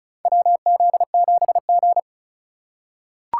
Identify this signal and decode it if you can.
Morse code, heard as a keyed tone, about 36 wpm, 700 Hz. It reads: WZ7G
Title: USA Call Signs - 4 Characters - 2x Character Spacing - 35wpm